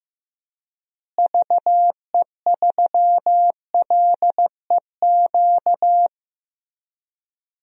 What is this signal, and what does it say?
Morse code, heard as a keyed tone, about 15 wpm, 700 Hz. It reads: VE3LEQ
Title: DXCC-100 Call Signs - 6 Characters - Repeated 3x - 15wpm